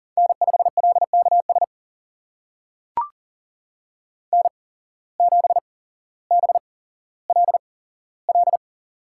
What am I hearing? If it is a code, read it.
Morse code, heard as a keyed tone, about 40 wpm, spaced as for 17 wpm, 700 Hz. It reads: N5LKS N7BLL
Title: USA Call Signs - 5 Characters - 8x Character Spacing - 40wpm